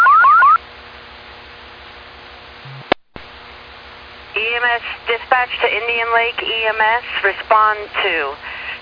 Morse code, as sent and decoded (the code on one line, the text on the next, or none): none